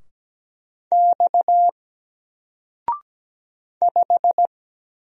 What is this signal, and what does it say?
Morse code, heard as a keyed tone, about 17 wpm, 700 Hz. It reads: X 5
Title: Single Character (Letter:Number:Punctuation:Prosign) 17wpm